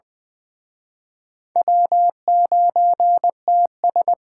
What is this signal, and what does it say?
Morse code, heard as a keyed tone, about 24 wpm, 700 Hz. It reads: W9TS
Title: Speed Racing - USA Call Signs 20wpm